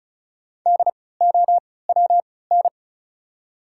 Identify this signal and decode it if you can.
Morse code, heard as a keyed tone, about 35 wpm, spaced as for 20 wpm, 700 Hz. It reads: DOWN
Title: 3x Character Spacing - Top 100 Words - 35wpm